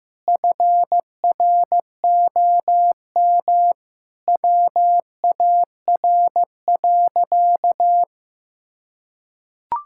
Morse code, heard as a keyed tone, about 15 wpm, 700 Hz. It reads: FROM WAR.
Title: Sentences from Top 1000 Words - Review Part 6 - 15wpm